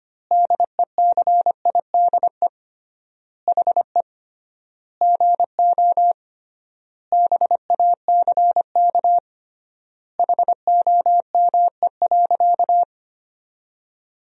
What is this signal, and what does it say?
Morse code, heard as a keyed tone, about 25 wpm, 700 Hz. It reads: DECIDE HE GO BACK HOME.
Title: 3x Word-Spacing - Sentences from Top 500 Words - 25wpm